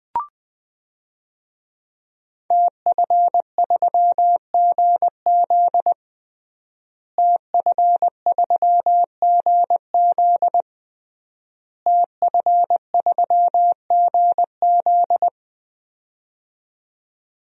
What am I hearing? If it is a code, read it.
Morse code, heard as a keyed tone, about 20 wpm, 700 Hz. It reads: TF3GZ TF3GZ TF3GZ
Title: DXCC-100 Call Signs - 5 Characters - Repeated 3x - 20wpm